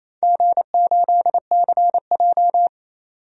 Morse code, heard as a keyed tone, about 28 wpm, 700 Hz. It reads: G8CJ
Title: DXCC-100 Call Signs - 4 Characters - 28wpm